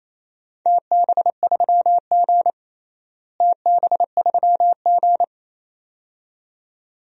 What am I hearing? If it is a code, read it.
Morse code, heard as a keyed tone, about 28 wpm, 700 Hz. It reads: TB3G TB3G
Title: DXCC-100 Call Signs - 4 Characters - Repeated 2x - 28wpm